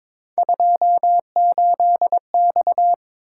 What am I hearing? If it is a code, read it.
Morse code, heard as a keyed tone, about 22 wpm, 700 Hz. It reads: 28X
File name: practice / 22wpm / Three Letter-Number Combinations - 22wpm